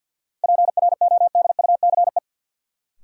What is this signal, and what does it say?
Morse code, heard as a keyed tone, about 53 wpm, 700 Hz. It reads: PRODUCE
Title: Sentences from Top 1000 Words - Review Part 3 - New Word Drill - 50wpm